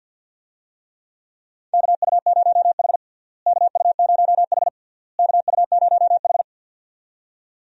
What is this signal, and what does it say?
Morse code, heard as a keyed tone, about 52 wpm, 700 Hz. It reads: KU0H KU0H KU0H
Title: USA Call Signs - 4 Characters - Repeated 3x - 50wpm